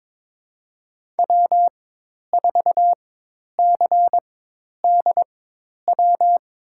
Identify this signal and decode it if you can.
Morse code, heard as a keyed tone, about 22 wpm, spaced as for 10 wpm, 700 Hz. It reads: W4CDW
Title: USA Call Signs - 5 Characters - 4x Character Spacing - 22wpm